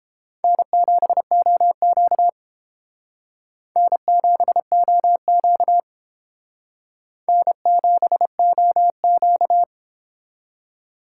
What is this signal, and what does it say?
Morse code, heard as a keyed tone, about 30 wpm, 700 Hz. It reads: N7OQ N7OQ N7OQ